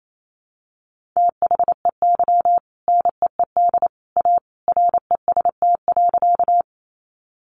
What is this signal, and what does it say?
Morse code, heard as a keyed tone, about 28 wpm, 700 Hz. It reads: THEY NEED A REST.
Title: Sentences from Top 500 Words 28wpm